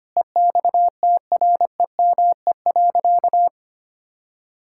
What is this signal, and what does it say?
Morse code, heard as a keyed tone, about 25 wpm, 700 Hz. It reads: EXTREME.
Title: Sentences from Top 1600 Words 25wpm